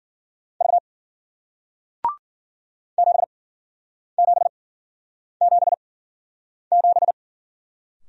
Morse code, heard as a keyed tone, about 52 wpm, spaced as for 9 wpm, 700 Hz. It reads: V 7777